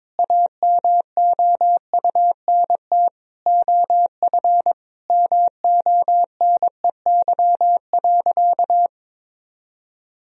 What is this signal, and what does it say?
Morse code, heard as a keyed tone, about 22 wpm, 700 Hz. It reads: AMOUNT OF MONEY.
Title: Sentences from Top 1000 Words - Review Part 4 - 22wpm